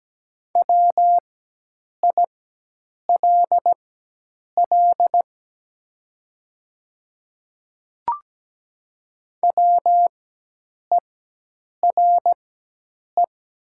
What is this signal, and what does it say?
Morse code, heard as a keyed tone, about 17 wpm, spaced as for 8 wpm, 700 Hz. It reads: WILL WERE
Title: 4x Character Spacing - Top 100 Words - 17wpm